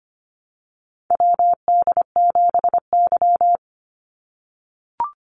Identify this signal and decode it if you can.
Morse code, heard as a keyed tone, about 25 wpm, 700 Hz. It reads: WD7Y
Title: NA SOTA Chaser Callsigns 25wpm